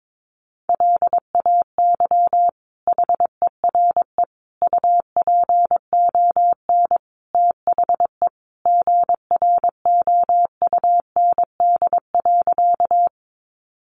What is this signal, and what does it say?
Morse code, heard as a keyed tone, about 22 wpm, 700 Hz. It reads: LAY HERE UPON THE GROUND.
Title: Sentences from Top 1100 Words 22wpm